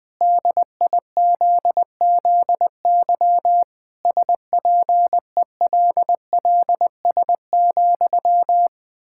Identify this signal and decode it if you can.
Morse code, heard as a keyed tone, about 20 wpm, 700 Hz. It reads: DIZZY SPELLS,
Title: Pangram Challenge v2 - 20wpm